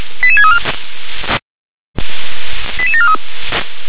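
Five-tone selective call (ZVEI), heard as quick runs of five tones, tones of about 70 ms. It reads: 88634 88743
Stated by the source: ZVEI five-tone call received on SDR.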